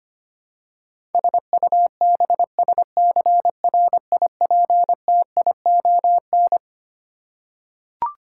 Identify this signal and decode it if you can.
Morse code, heard as a keyed tone, about 25 wpm, 700 Hz. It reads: SUBSCRIPTION
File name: Twelve Letter Words - 25wpm